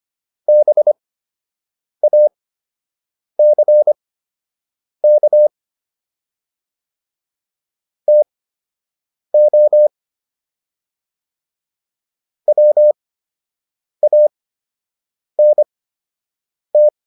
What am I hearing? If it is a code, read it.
Morse code, heard as a keyed tone, about 25 wpm, spaced as for 7 wpm, 600 Hz. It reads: BACK TO WANT